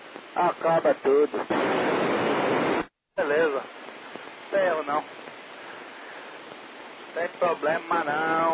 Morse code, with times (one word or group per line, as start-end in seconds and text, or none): none